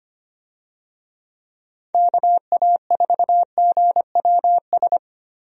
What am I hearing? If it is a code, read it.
Morse code, heard as a keyed tone, about 25 wpm, 700 Hz. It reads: KA4GWS
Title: USA Call Signs - 6 Characters - Repeated 2x - 25wpm